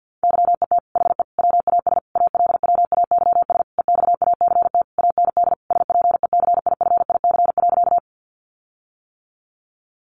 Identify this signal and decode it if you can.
Morse code, heard as a keyed tone, about 50 wpm, 700 Hz. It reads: YET HE WAS ALWAYS EXACT AND SPECIFIC.
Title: Sentences from Top 1300 Words 50wpm